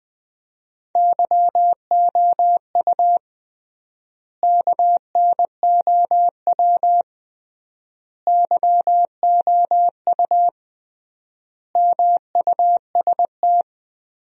Morse code, heard as a keyed tone, about 19 wpm, 700 Hz. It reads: YOU KNOW YOU MUST
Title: 3x Word-Spacing - Sentences from Top 500 Words - 20wpm